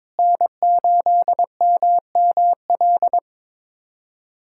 Morse code, heard as a keyed tone, about 22 wpm, 700 Hz. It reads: N8MML
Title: USA Callsign DE Callsign 22wpm